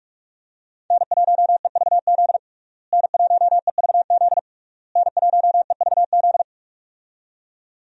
Morse code, heard as a keyed tone, about 46 wpm, 700 Hz. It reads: N1EVZ N1EVZ N1EVZ
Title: USA Call Signs - 5 Characters - Repeated 3x - 45wpm